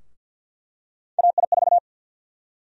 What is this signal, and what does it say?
Morse code, heard as a keyed tone, about 49 wpm, 700 Hz. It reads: AI4
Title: USA Call Sign Prefixes - Repeated 2x - 50wpm